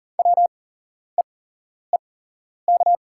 Morse code, heard as a keyed tone, about 40 wpm, spaced as for 11 wpm, 690 Hz. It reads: WEEK
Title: 8x Character Spacing - Top 500 Words - 40wpm